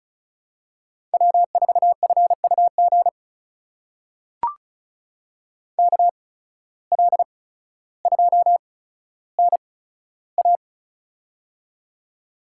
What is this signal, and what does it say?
Morse code, heard as a keyed tone, about 36 wpm, spaced as for 15 wpm, 700 Hz. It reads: W4FUG KL2NA
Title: USA Call Signs - 8x Character Spacing - 35wpm